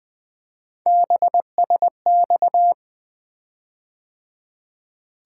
Morse code, heard as a keyed tone, about 20 wpm, 700 Hz. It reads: BSX
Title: Three Letter-Number Combinations - 20wpm